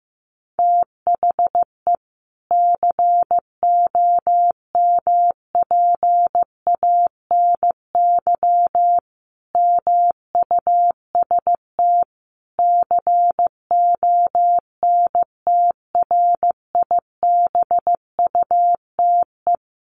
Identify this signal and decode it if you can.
Morse code, heard as a keyed tone, about 15 wpm, 700 Hz. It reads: THE COMPANY MUST CONTRIBUTE
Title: Sentences from Top 1400 Words 15wpm